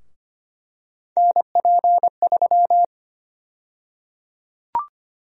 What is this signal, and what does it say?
Morse code, heard as a keyed tone, about 25 wpm, 700 Hz. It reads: NP3
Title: USA Call Sign Prefixes - Repeated 2x - 25wpm